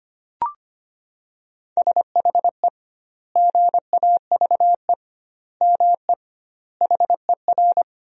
Morse code, heard as a keyed tone, about 25 wpm, 700 Hz. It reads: SHE GAVE ME HER
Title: 2x Word-Spacing - Sentences from Top 500 Words - 25wpm